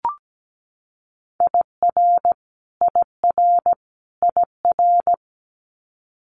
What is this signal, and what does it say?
Morse code, heard as a keyed tone, about 17 wpm, 700 Hz. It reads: IR IR IR
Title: Multi-letter Phonemes Repeated 3x 17wpm